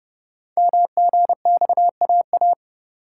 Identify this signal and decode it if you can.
Morse code, heard as a keyed tone, about 30 wpm, 700 Hz. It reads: MGXAA
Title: Code Groups - 5 Letters - 30wpm